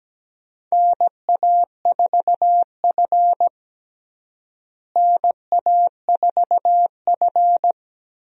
Morse code, heard as a keyed tone, about 17 wpm, 700 Hz. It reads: NA4F NA4F
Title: USA Call Signs - 4 Characters - Repeated 2x - 17wpm